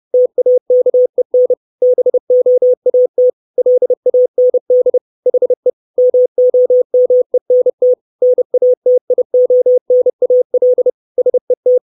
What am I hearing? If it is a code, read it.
Morse code, heard as a keyed tone, about 30 wpm, 500 Hz. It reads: TAKEN BOAT LAND HE MOMENT NATIONAL SET
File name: 632 Words 30WPM